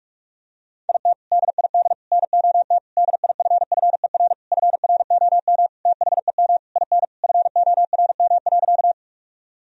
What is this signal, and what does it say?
Morse code, heard as a keyed tone, about 46 wpm, 700 Hz. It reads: IT DID NOT DIFFER FROM THEM IN FORM.
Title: Sentences from Top 1000 Words - Review Part 1 - 45wpm